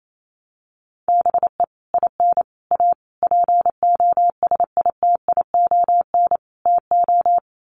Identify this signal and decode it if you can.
Morse code, heard as a keyed tone, about 28 wpm, 700 Hz. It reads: BE IN A POSITION TO
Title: Sets of 5 Words Encore2 28wpm